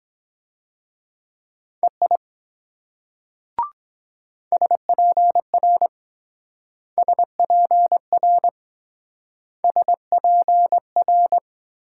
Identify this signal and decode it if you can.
Morse code, heard as a keyed tone, about 22 wpm, 700 Hz. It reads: EI SPR SPR SPR